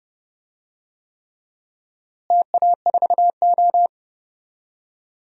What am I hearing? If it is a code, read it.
Morse code, heard as a keyed tone, about 32 wpm, 700 Hz. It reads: TA4O
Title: DXCC-100 Call Signs - 4 Characters - Repeated 3x - 30wpm